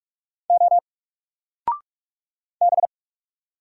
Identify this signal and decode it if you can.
Morse code, heard as a keyed tone, about 45 wpm, 700 Hz. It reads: O B
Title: Single Character (Letter:Number:Punctuation:Prosign) Alt 45wpm